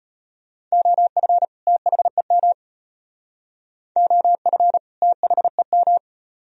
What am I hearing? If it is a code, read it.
Morse code, heard as a keyed tone, about 33 wpm, 700 Hz. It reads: OF THEM OF THEM